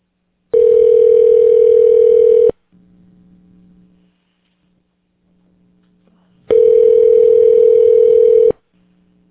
Ringback tone (North America), 440 + 480 Hz.